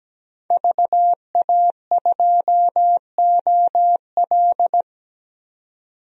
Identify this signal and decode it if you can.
Morse code, heard as a keyed tone, about 17 wpm, 700 Hz. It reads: VA2OL
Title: DXCC-100 Call Signs - 5 Characters - 17wpm